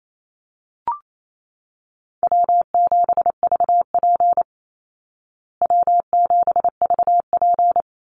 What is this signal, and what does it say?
Morse code, heard as a keyed tone, about 28 wpm, 700 Hz. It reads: W7VP W7VP
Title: Call Signs Repeated 3x - 28wpm